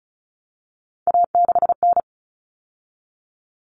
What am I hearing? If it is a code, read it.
Morse code, heard as a keyed tone, about 34 wpm, 700 Hz. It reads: A6N
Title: All Cut Numbers - 3-Digit 35wpm